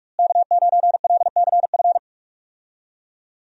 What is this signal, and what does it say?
Morse code, heard as a keyed tone, about 46 wpm, 700 Hz. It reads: K9LCF
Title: USA Call Signs - 5 Characters - Repeated 2x - 45wpm